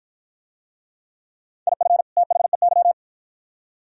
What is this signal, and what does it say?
Morse code, heard as a keyed tone, about 51 wpm, 700 Hz. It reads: IF THEY